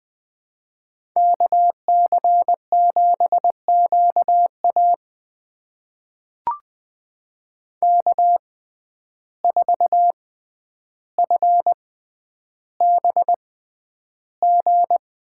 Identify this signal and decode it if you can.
Morse code, heard as a keyed tone, about 20 wpm, spaced as for 10 wpm, 700 Hz. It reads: KC7QA K4FBG
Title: USA Call Signs - 5 Characters - 6x Character Spacing - 20wpm